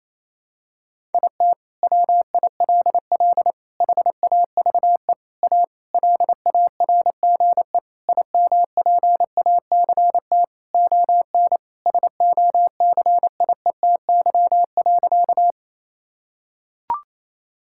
Morse code, heard as a keyed tone, about 27 wpm, 700 Hz. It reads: IT WILL HAVE A LARGE IMPACT ON SOCIETY.
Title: Sentences from Top 1000 Words - Review Part 6 - 28wpm